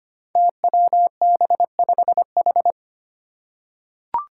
Morse code, heard as a keyed tone, about 25 wpm, 700 Hz. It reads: TWB5H
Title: Five Challenging Letter-Number Combinations 25wpm